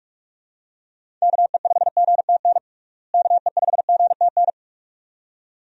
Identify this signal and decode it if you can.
Morse code, heard as a keyed tone, about 43 wpm, 700 Hz. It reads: KE5GTN KE5GTN
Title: USA Call Signs - 6 Characters - Repeated 2x - 45wpm